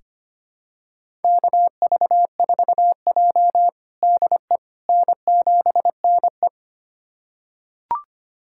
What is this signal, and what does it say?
Morse code, heard as a keyed tone, about 25 wpm, 700 Hz. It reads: KV4J DE N7NE
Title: USA Callsign DE Callsign - 4 Characters - 25wpm